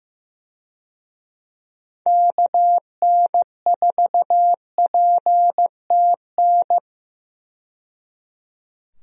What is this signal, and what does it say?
Morse code, heard as a keyed tone, about 16 wpm, 700 Hz. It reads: KN4PTN
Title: USA Call Signs - 6 Characters - 15wpm